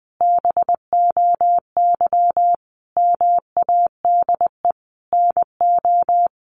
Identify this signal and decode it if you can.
Morse code, heard as a keyed tone, about 20 wpm, 700 Hz. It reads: BOY MADE NO
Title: Sentences from Top 1400 Words 20wpm